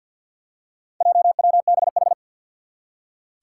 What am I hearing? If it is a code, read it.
Morse code, heard as a keyed tone, about 48 wpm, 700 Hz. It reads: JWBH
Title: Four Challenging Letter-Number Combinations 50wpm